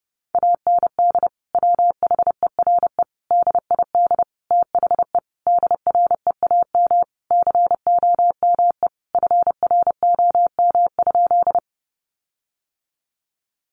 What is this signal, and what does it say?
Morse code, heard as a keyed tone, about 30 wpm, 700 Hz. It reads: AND WHERE DID THE DREAM COME FROM?
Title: Sentences from Top 1500 Words 30wpm